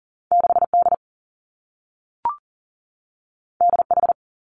Morse code, heard as a keyed tone, about 40 wpm, 700 Hz. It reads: 6 D D H
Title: Two Challenging Letter-Number Combinations 40wpm